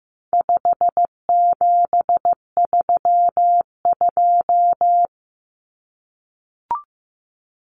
Morse code, heard as a keyed tone, about 15 wpm, 700 Hz. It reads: 5732
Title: Numbers - 4-Digit 15wpm